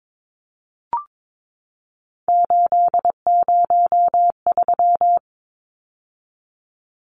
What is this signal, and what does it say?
Morse code, heard as a keyed tone, about 22 wpm, 700 Hz. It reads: 803
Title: Numbers - 3-Digit 22wpm